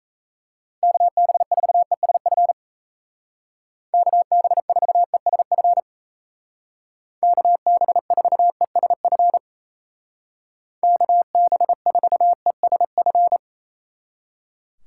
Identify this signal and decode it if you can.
Morse code, heard as a keyed tone, about 36 wpm, 700 Hz. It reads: KB4ESF KB4ESF KB4ESF KB4ESF